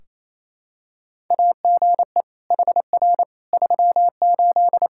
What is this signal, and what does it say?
Morse code, heard as a keyed tone, about 27 wpm, 700 Hz. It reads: AGE HR 38
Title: K1USN SST QSO Practice 28wpm